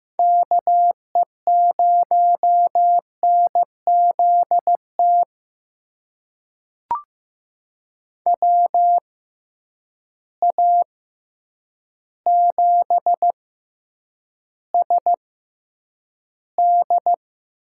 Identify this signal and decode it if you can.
Morse code, heard as a keyed tone, about 15 wpm, spaced as for 8 wpm, 700 Hz. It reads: KE0NZT WA7SD
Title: USA Call Signs - 6 Characters - 6x Character Spacing - 15wpm